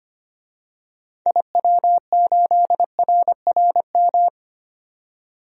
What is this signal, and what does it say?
Morse code, heard as a keyed tone, about 25 wpm, 700 Hz. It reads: IW8RRM
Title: DXCC-100 Call Signs - 6 Characters - 25wpm